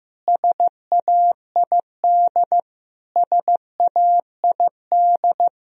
Morse code, heard as a keyed tone, about 15 wpm, 700 Hz. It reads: SAID SAID
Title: Top 100 Words Repeated 15wpm